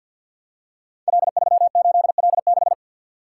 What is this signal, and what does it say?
Morse code, heard as a keyed tone, about 52 wpm, 700 Hz. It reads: L38L6
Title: Code Groups - 5 Letter:Numbers - 50wpm